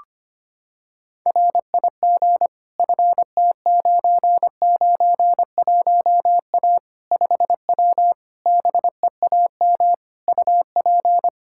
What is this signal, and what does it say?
Morse code, heard as a keyed tone, about 25 wpm, 700 Hz. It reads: RIG FT991A 5W BEAM UP
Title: Practice QSO - Exchange 2 - Info - 25wpm